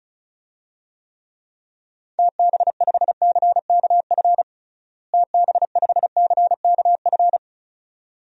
Morse code, heard as a keyed tone, about 36 wpm, 700 Hz. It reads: TB5CKF TB5CKF
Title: DXCC-100 Call Signs - 6 Characters - Repeated 2x - 35wpm